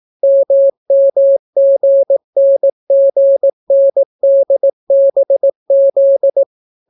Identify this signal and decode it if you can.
Morse code, heard as a keyed tone, about 18 wpm, 550 Hz. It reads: MMGNGNDBZ